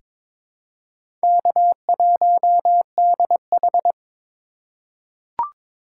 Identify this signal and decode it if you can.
Morse code, heard as a keyed tone, about 22 wpm, 700 Hz. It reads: K1DH
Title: USA Call Signs - 4 Characters - 3x Character Spacing - 22wpm